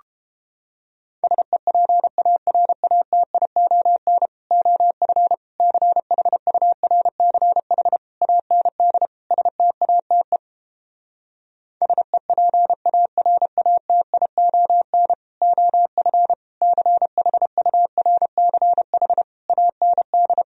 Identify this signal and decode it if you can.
Morse code, heard as a keyed tone, about 31 wpm, 700 Hz. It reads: SEPARATION OF CHURCH AND STATE SEPARATION OF CHURCH AND